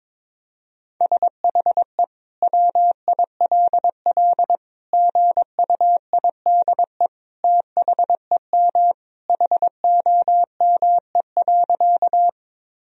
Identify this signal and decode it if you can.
Morse code, heard as a keyed tone, about 22 wpm, 700 Hz. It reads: SHE WILL GUIDE THEM HOME.
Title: Sentences from Top 1000 Words - Review Part 3 - 22wpm